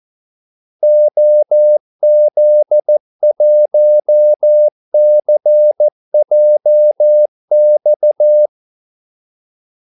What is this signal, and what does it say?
Morse code, heard as a keyed tone, about 14 wpm, 600 Hz. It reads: OZ1CJX
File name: Callsigns-14wpm